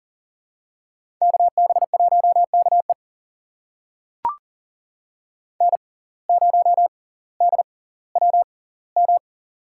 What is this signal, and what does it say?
Morse code, heard as a keyed tone, about 40 wpm, spaced as for 20 wpm, 700 Hz. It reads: KB1KE N0DWM
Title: USA Call Signs - 5 Characters - 6x Character Spacing - 40wpm